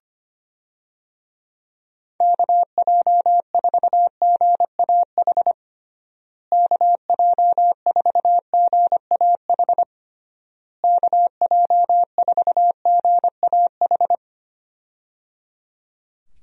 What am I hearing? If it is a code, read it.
Morse code, heard as a keyed tone, about 25 wpm, 700 Hz. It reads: KJ4GAH KJ4GAH KJ4GAH
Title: USA Call Signs - 6 Characters - Repeated 3x - 25wpm